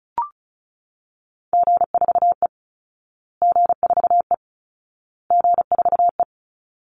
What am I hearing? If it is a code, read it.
Morse code, heard as a keyed tone, about 34 wpm, 700 Hz. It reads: G4E G4E G4E
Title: Call Signs Repeated 2x - 35wpm